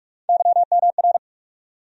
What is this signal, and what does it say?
Morse code, heard as a keyed tone, about 43 wpm, 700 Hz. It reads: YMR
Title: USA Call Sign Postfixes - 3 Letters 45wpm